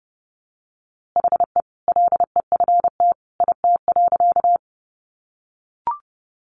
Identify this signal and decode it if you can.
Morse code, heard as a keyed tone, about 30 wpm, 700 Hz. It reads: HE LEFT IT.
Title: A Sound Of Thunder 30wpm